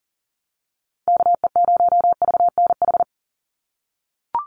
Morse code, heard as a keyed tone, about 37 wpm, 690 Hz. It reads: KE0VNH
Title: USA Call Signs 40wpm